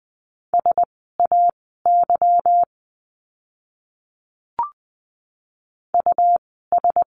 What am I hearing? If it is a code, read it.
Morse code, heard as a keyed tone, about 20 wpm, spaced as for 15 wpm, 700 Hz. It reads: SAY US
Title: 237 Lesson 38f - Letter q - Words 20wpm